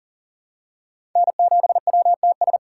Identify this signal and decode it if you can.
Morse code, heard as a keyed tone, about 40 wpm, 700 Hz. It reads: N7WTS
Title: USA Call Signs - 5 Characters - 4x Character Spacing - 40wpm